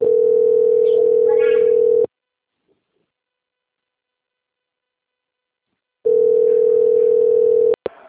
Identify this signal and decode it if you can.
Ringback tone (North America), 440 + 480 Hz.